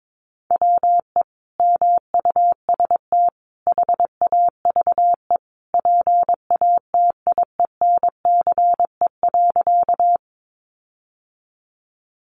Morse code, heard as a keyed tone, about 22 wpm, 700 Hz. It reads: WE MUST HAVE PATIENCE.
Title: Sentences from Top 1300 Words 22wpm